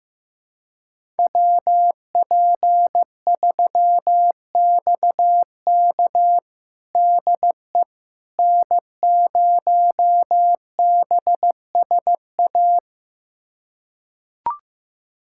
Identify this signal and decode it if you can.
Morse code, heard as a keyed tone, about 15 wpm, 700 Hz. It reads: WP3XK DE N0BSA
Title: USA Callsign DE Callsign - 5 Characters - 15wpm